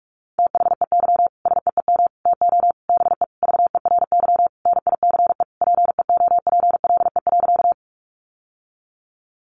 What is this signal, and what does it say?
Morse code, heard as a keyed tone, about 44 wpm, 700 Hz. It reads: THEY SEEM TO BE VERY NICE PEOPLE.
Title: Sentences from Top 1100 Words 45wpm